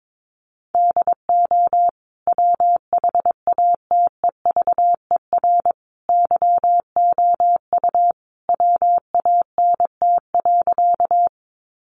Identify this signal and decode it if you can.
Morse code, heard as a keyed tone, about 22 wpm, 700 Hz. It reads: DO WHATEVER YOU WANT.
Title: Sentences from Top 1100 Words 22wpm